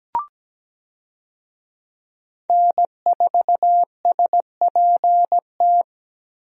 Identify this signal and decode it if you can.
Morse code, heard as a keyed tone, about 17 wpm, 700 Hz. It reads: N4SPT
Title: USA Call Signs - 5 Characters - Repeated 2x - 17wpm